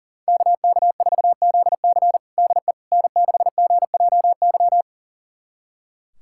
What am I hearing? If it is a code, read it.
Morse code, heard as a keyed tone, about 37 wpm, 700 Hz. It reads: KK4ZC DE N6GJY
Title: USA Callsign DE Callsign - 5 Characters - 40wpm